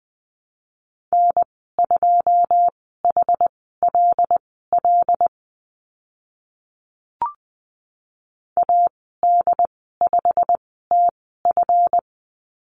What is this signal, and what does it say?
Morse code, heard as a keyed tone, about 20 wpm, spaced as for 15 wpm, 700 Hz. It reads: N2HLL AD5TF
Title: 140 Lesson 23g - Letter f - Call Signs 20wpm